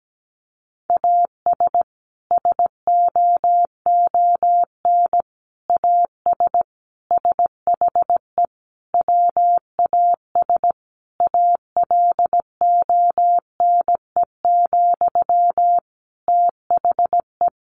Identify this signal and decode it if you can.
Morse code, heard as a keyed tone, about 17 wpm, 700 Hz. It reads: AS SOON AS SHE WAS ALONE, THE
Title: Sentences from Top 1100 Words 17wpm